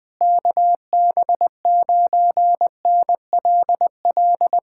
Morse code, heard as a keyed tone, about 21 wpm, 700 Hz. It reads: KB9NLL
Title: Speed-Racing Common QSO Phrases - 20wpm